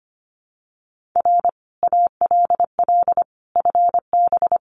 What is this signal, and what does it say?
Morse code, heard as a keyed tone, about 25 wpm, 700 Hz. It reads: R ALL FB
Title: Common QSO Phrases 25wpm